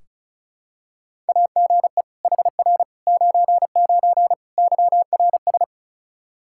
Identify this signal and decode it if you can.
Morse code, heard as a keyed tone, about 36 wpm, 700 Hz. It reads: AGE HR 99 YRS
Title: Practice QSO - Exchange 3 - More Info - 35wpm